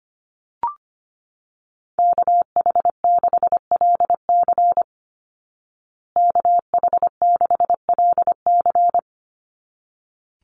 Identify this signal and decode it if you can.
Morse code, heard as a keyed tone, about 25 wpm, 700 Hz. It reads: KH6LC KH6LC
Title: Call Signs Repeated 3x - 25wpm